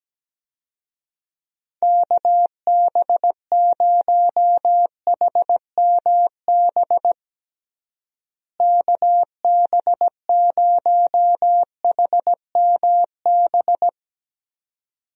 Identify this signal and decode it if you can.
Morse code, heard as a keyed tone, about 17 wpm, 700 Hz. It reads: KB0HMB KB0HMB
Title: USA Call Signs - 6 Characters - Repeated 2x - 17wpm